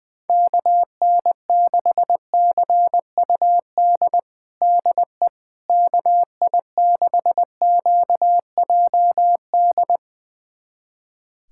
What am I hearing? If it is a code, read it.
Morse code, heard as a keyed tone, about 20 wpm, 700 Hz. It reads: KN6CUD DE KI6QJD
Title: USA Callsign DE Callsign - 6 Characters - 20wpm